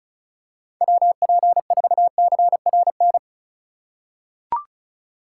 Morse code, heard as a keyed tone, about 35 wpm, 690 Hz. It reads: WP4CRN
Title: USA Call Signs - 6 Characters - 3x Character Spacing - 35wpm